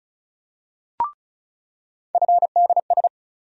Morse code, heard as a keyed tone, about 36 wpm, 700 Hz. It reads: FDS
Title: USA Call Sign Postfixes - 3 Letters 35wpm